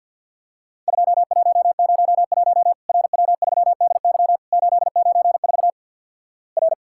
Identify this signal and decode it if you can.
Morse code, heard as a keyed tone, about 48 wpm, 700 Hz. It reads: 2101 RW3DY 894 R
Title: Worked All Europe DX Contest - QTC Exchange with Repeats - 50wpm